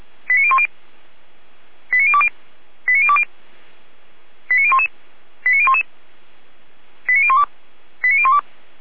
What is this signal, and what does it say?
Five-tone selective call (ZVEI), heard as quick runs of five tones, tones of about 70 ms. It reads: 89010 89020 89020 89011 89011 89012 89012